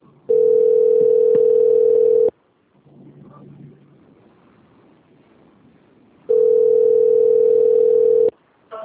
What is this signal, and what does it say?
Ringback tone (North America), 440 + 480 Hz.